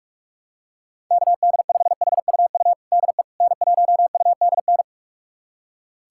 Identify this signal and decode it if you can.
Morse code, heard as a keyed tone, about 46 wpm, 700 Hz. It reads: KD5HUU DE N1UDN
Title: USA Callsign DE Callsign 45wpm